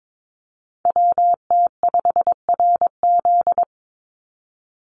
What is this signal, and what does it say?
Morse code, heard as a keyed tone, about 22 wpm, 700 Hz. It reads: WT5RZ
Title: Call Signs Repeated 3x - 22wpm